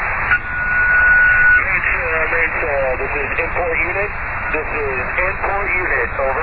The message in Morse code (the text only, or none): none